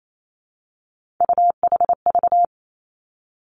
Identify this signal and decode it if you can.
Morse code, heard as a keyed tone, about 28 wpm, 700 Hz. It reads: UHV
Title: Three Challenging Letter-Number Combinations 28wpm